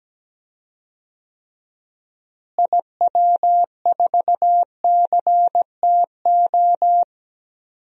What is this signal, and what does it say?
Morse code, heard as a keyed tone, about 17 wpm, 700 Hz. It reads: IW4CTO
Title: DXCC-100 Call Signs - 6 Characters - Repeated 3x - 17wpm